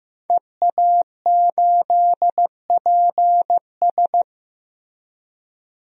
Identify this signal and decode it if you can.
Morse code, heard as a keyed tone, about 15 wpm, 700 Hz. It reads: EA8PS
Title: DXCC-100 Call Signs - 5 Characters - Repeated 3x - 15wpm